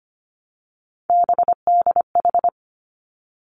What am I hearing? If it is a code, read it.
Morse code, heard as a keyed tone, about 25 wpm, 700 Hz. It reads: BDH
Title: Three Challenging Letter-Number Combinations 25wpm